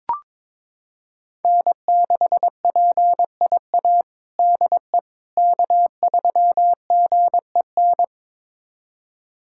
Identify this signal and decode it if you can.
Morse code, heard as a keyed tone, about 22 wpm, 700 Hz. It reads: N6PIA DE K3GEN
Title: USA Callsign DE Callsign - 5 Characters - 22wpm